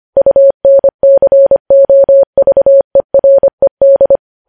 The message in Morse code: UNCOVERED

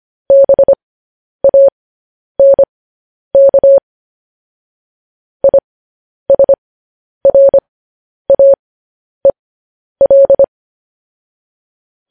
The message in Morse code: BANK ISRAEL

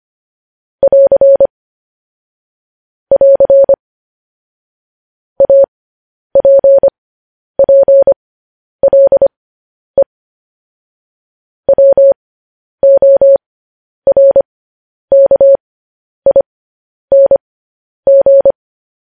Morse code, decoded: <AR> <AR> APPLE WORKING

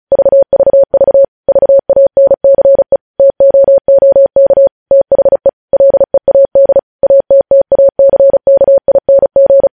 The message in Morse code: VVV VANCE TOOK THE LEAD ATTACKING

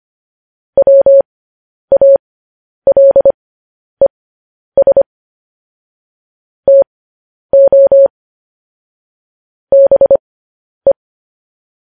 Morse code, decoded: WALES TO BE

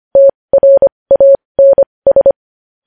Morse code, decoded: TRANS

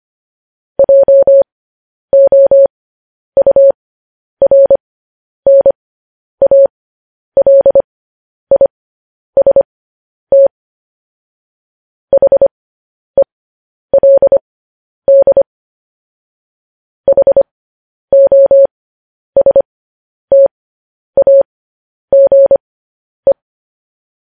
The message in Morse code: JOURNALIST HELD HOSTAGE

